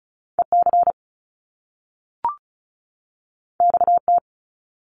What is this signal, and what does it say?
Morse code, heard as a keyed tone, about 36 wpm, 700 Hz. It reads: EC XT
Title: Top Two Letter Combinations in Words 35wpm